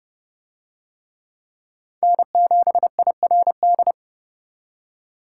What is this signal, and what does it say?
Morse code, heard as a keyed tone, about 30 wpm, 700 Hz. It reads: N7IRD